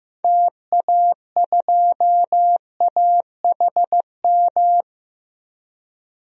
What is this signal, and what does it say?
Morse code, heard as a keyed tone, about 15 wpm, 700 Hz. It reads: TA2AHM
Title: DXCC-100 Call Signs - 6 Characters - 15wpm